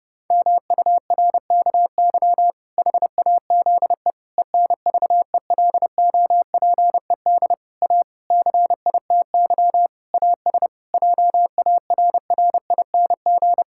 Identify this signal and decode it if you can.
Morse code, heard as a keyed tone, about 28 wpm, 700 Hz. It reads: MURKY HAZE ENVELOPED A CITY AS JARRING
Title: Pangram Challenge v2 - 30wpm